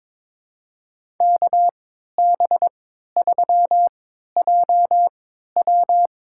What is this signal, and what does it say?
Morse code, heard as a keyed tone, about 22 wpm, spaced as for 13 wpm, 700 Hz. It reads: KB3JW
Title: USA Call Signs - 3x Character Spacing - 22wpm